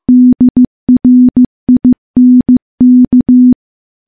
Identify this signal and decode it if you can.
Morse code, heard as a keyed tone, about 15 wpm, 250 Hz. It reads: DRINK
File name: morse / morse-training-15WPM